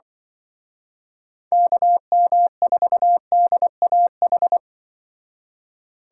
Morse code, heard as a keyed tone, about 24 wpm, 700 Hz. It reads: KM4DAH
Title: Speed Racing - USA 2x3 Call Signs 20wpm